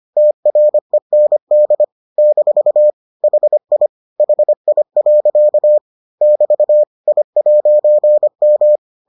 Morse code, heard as a keyed tone, about 25 wpm, 600 Hz. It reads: TREND - HI HI. <BT> I'M